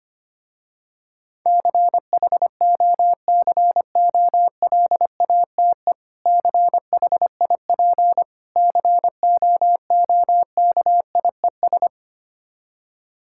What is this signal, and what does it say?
Morse code, heard as a keyed tone, about 25 wpm, 700 Hz. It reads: CHOCOLATE CHIP COOKIES
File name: practice / 25wpm / Difficult Words with Hints - 25wpm